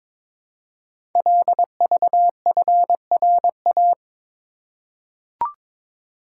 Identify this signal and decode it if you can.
Morse code, heard as a keyed tone, about 22 wpm, 700 Hz. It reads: LVFRA
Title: Code Groups - 5 Letters - 22wpm